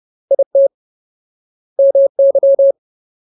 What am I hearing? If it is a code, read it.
Morse code, heard as a keyed tone, about 30 wpm, 550 Hz. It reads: IT MY
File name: Ultimate Word List / 30WPM Ultimate